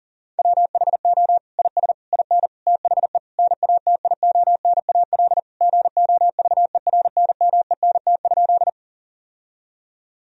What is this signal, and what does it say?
Morse code, heard as a keyed tone, about 40 wpm, 700 Hz. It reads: WHO IS IN THE NATIONAL GOVERNMENT?
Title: Sentences from Top 1000 Words - Review Part 4 - 40wpm